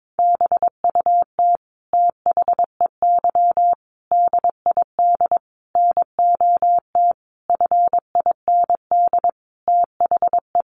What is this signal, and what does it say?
Morse code, heard as a keyed tone, about 22 wpm, 700 Hz. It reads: BUT THEY DID NOT FIND THE
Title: Sentences from Top 1400 Words 22wpm